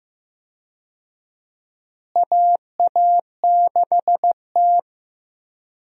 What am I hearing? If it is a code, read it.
Morse code, heard as a keyed tone, about 16 wpm, 700 Hz. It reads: AA6T
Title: USA Call Signs - 4 Characters - Repeated 2x - 15wpm